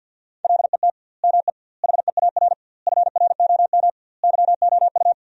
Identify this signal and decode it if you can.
Morse code, heard as a keyed tone, about 50 wpm, 700 Hz. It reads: LET ME HEAR FROM YOU
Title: 2x Word-Spacing - Sentences from Top 400 Words - 50wpm